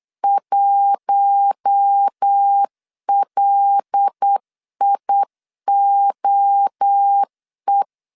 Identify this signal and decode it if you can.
Morse code, heard as a keyed tone, about 8 wpm, 800 Hz. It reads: 1LIOE
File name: 8wpm